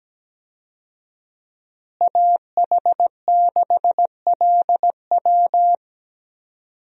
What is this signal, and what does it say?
Morse code, heard as a keyed tone, about 17 wpm, 700 Hz. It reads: AH6LW
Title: USA Call Signs - 5 Characters - Repeated 2x - 17wpm